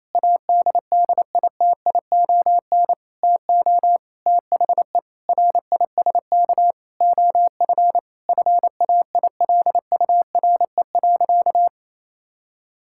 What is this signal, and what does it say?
Morse code, heard as a keyed tone, about 28 wpm, 700 Hz. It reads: ADDITION TO THE RISK OF FAILURE.
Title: Sentences from Top 1000 Words - Review Part 6 - 28wpm